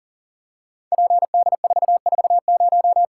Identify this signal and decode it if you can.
Morse code, heard as a keyed tone, about 38 wpm, 700 Hz. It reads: PD440
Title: Code Groups - 5 Letter:Numbers - 40wpm